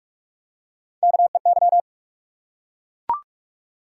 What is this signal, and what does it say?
Morse code, heard as a keyed tone, about 44 wpm, 700 Hz. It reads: KEY
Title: Three Letter Words - 45wpm